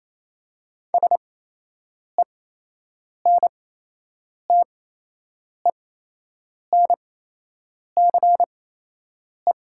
Morse code, heard as a keyed tone, about 28 wpm, spaced as for 8 wpm, 700 Hz. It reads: SENTENCE
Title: 8x Character Spacing - Top 300 Words - 28wpm